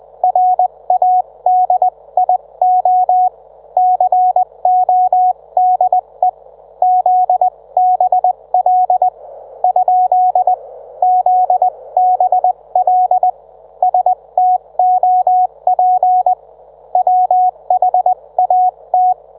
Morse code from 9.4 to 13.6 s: ? ZBL